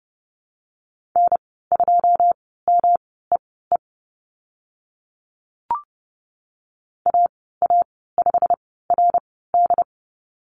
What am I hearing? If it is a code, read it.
Morse code, heard as a keyed tone, about 30 wpm, spaced as for 17 wpm, 700 Hz. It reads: N2MEE AA5RD
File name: Lesson 19g - Letter w - Call Signs 30wpm